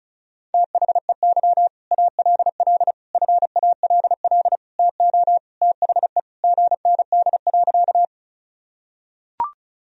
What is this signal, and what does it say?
Morse code, heard as a keyed tone, about 36 wpm, 700 Hz. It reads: THEY ALL FALL TO THE GND.
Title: Sentences from Top 500 Words and Abbreviations 35wpm